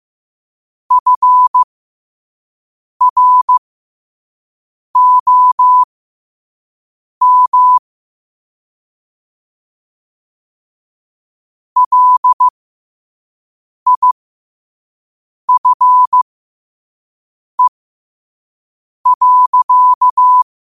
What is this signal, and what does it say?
Morse code, heard as a keyed tone, about 15 wpm, spaced as for 5 wpm, 1000 Hz. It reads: FROM LIFE.
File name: qotdfiles / 5wpm